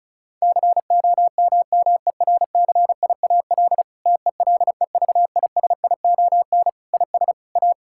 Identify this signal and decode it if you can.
Morse code, heard as a keyed tone, about 34 wpm, 690 Hz. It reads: COMMERCIAL TELEVISION IS A
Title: Sentences from Top 1000 Words - Review Part 6 - 35wpm